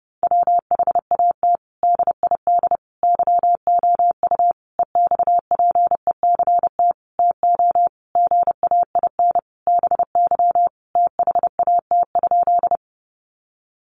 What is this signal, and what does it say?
Morse code, heard as a keyed tone, about 30 wpm, 700 Hz. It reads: WHAT DID YOU EXPECT TO GAIN BY THAT?
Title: Sentences from Top 1100 Words 30wpm